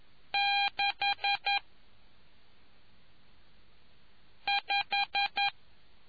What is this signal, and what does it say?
Morse code, heard as a keyed tone, about 11 wpm, 800 Hz. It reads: 6 5